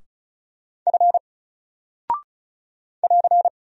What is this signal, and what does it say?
Morse code, heard as a keyed tone, about 36 wpm, 700 Hz. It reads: F <AR>
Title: Single Character (Letter:Number:Punctuation:Prosign) 35wpm